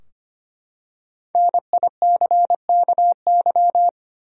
Morse code, heard as a keyed tone, about 25 wpm, 700 Hz. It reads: NICKY
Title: USA Common Names - Max 5 Letters - 25wpm